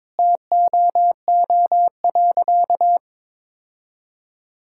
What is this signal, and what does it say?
Morse code, heard as a keyed tone, about 22 wpm, 700 Hz. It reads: TOO.
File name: practice / 22wpm / Sentences with 8-Word Phrases - 22wpm